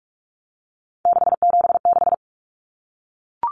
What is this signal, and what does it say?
Morse code, heard as a keyed tone, about 44 wpm, 700 Hz. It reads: 676
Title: Numbers - 3-Digit 45wpm